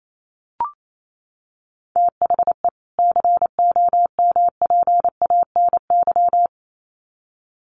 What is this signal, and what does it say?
Morse code, heard as a keyed tone, about 28 wpm, 700 Hz. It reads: THE COMPANY
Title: Sets of 2 Words 28wpm